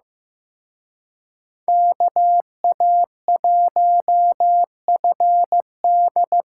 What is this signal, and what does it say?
Morse code, heard as a keyed tone, about 18 wpm, 700 Hz. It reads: KA1FD
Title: Speed Racing - USA 2x2 Call Signs 15wpm